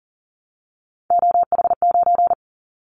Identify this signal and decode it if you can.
Morse code, heard as a keyed tone, about 42 wpm, 700 Hz. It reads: OH9
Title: Callsign Prefixes for Top 50 DXCC - 40wpm